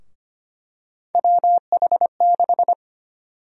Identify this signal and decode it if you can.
Morse code, heard as a keyed tone, about 25 wpm, 700 Hz. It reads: WH6
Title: USA Call Sign Prefixes - Repeated 2x - 25wpm